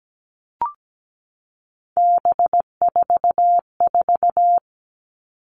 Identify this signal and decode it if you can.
Morse code, heard as a keyed tone, about 17 wpm, 700 Hz. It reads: B44
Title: All Cut Numbers - 3-Digit 17wpm